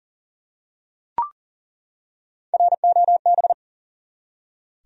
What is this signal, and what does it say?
Morse code, heard as a keyed tone, about 40 wpm, 700 Hz. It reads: ROB
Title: USA Common Names - Max 3 Letters - 40wpm